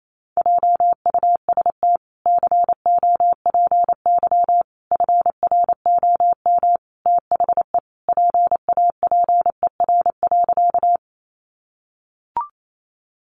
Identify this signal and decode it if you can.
Morse code, heard as a keyed tone, about 28 wpm, 700 Hz. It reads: JUST COPY FROM THE PAPER.
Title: Sentences from Top 1400 Words 28wpm